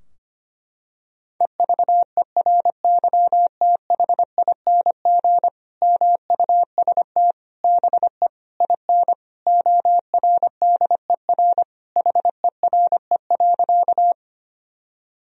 Morse code, heard as a keyed tone, about 25 wpm, 700 Hz. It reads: EVERYTHING MUST BE IN ORDER HERE.
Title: Sentences from Top 1000 Words - Review Part 2 - 25wpm